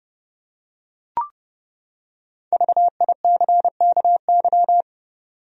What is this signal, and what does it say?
Morse code, heard as a keyed tone, about 30 wpm, 700 Hz. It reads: VICKY
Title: USA Common Names - Max 5 Letters - 30wpm